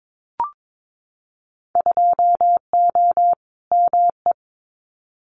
Morse code, heard as a keyed tone, about 22 wpm, 700 Hz. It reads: 2O ME
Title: Winter Field Day Exchanges 22wpm